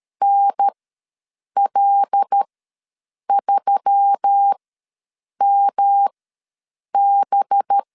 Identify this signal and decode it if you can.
Morse code, heard as a keyed tone, about 13 wpm, spaced as for 7 wpm, 800 Hz. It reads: NL3MB
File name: LEZIONI / 5 LBVC / 5 LB ws 12wpm 3034